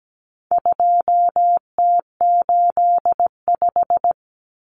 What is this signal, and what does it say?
Morse code, heard as a keyed tone, about 17 wpm, 700 Hz. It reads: 2T85
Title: Common Cut Numbers - 4-Digit 17wpm